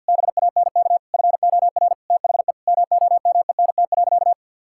Morse code, heard as a keyed tone, about 52 wpm, 700 Hz. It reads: BANK FOR THE MOMENT.
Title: Sentences from Top 1000 Words - Review Part 6 - 50wpm